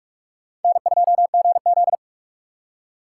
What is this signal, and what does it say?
Morse code, heard as a keyed tone, about 47 wpm, 700 Hz. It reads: N2GZ
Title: USA Call Signs - 6x Character Spacing - 45wpm